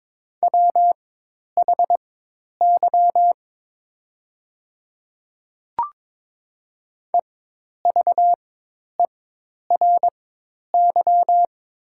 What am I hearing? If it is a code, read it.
Morse code, heard as a keyed tone, about 21 wpm, spaced as for 10 wpm, 700 Hz. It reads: WHY EVERY
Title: 4x Character Spacing - Top 200 Words - 22wpm